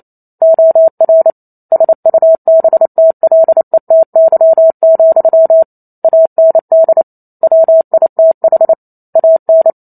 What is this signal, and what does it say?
Morse code, heard as a keyed tone, about 28 wpm, 650 Hz. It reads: OR SUBTLETY, AND WITH AN